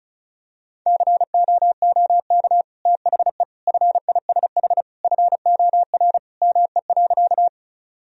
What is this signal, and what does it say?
Morse code, heard as a keyed tone, about 34 wpm, 700 Hz. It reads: COOK THE FISH FOR ME.
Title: Sentences from Top 1000 Words - Review Part 4 - 35wpm